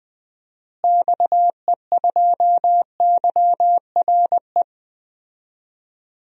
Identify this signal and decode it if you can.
Morse code, heard as a keyed tone, about 21 wpm, 700 Hz. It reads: XE2YRE
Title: DXCC-100 Call Signs - 6 Characters - Repeated 2x - 20wpm